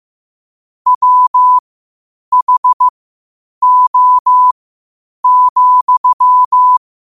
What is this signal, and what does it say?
Morse code, heard as a keyed tone, about 15 wpm, spaced as for 8 wpm, 1000 Hz. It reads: WHO,
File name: qotdfiles / 8wpm